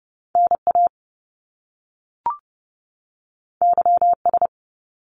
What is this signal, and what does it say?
Morse code, heard as a keyed tone, about 30 wpm, 700 Hz. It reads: NA YS
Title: Top Two Letter Combinations in Words 30wpm